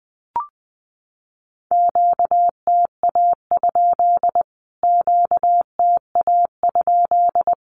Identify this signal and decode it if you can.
Morse code, heard as a keyed tone, about 19 wpm, 700 Hz. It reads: QTA? QTA?
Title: Q-Codes Repeated 3x 20wpm